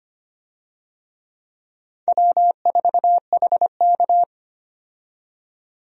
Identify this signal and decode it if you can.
Morse code, heard as a keyed tone, about 25 wpm, 700 Hz. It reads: W4HK
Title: USA Call Signs - 4 Characters - 25wpm